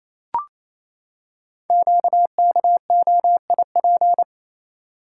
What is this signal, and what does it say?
Morse code, heard as a keyed tone, about 27 wpm, 700 Hz. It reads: QKOIP
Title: Code Groups - 5 Letters - 28wpm